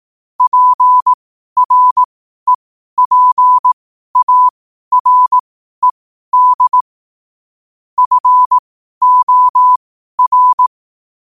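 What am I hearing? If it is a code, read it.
Morse code, heard as a keyed tone, about 18 wpm, spaced as for 12 wpm, 1000 Hz. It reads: PREPARED FOR
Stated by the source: Morse code at 12 wpm.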